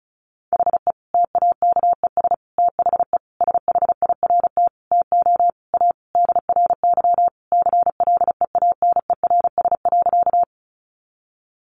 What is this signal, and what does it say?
Morse code, heard as a keyed tone, about 34 wpm, 700 Hz. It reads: HE TAKES THE SHIRT TO A DRY CLEANERS.
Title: Additional Easy Phrases 35wpm